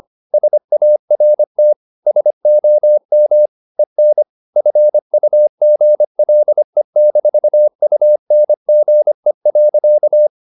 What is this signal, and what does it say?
Morse code, heard as a keyed tone, about 25 wpm, 600 Hz. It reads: SART SOM EN FUGLE-UNGE.
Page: Ferie 25 wpm